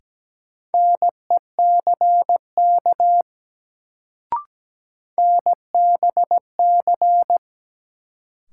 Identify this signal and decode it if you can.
Morse code, heard as a keyed tone, about 17 wpm, 700 Hz. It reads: NECK NBC
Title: Sets of 5 Words Encore - New Word Drill - 17wpm